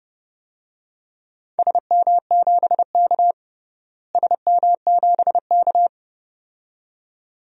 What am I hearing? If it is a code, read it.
Morse code, heard as a keyed tone, about 30 wpm, 700 Hz. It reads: SM7K SM7K
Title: DXCC-100 Call Signs - 4 Characters - Repeated 2x - 30wpm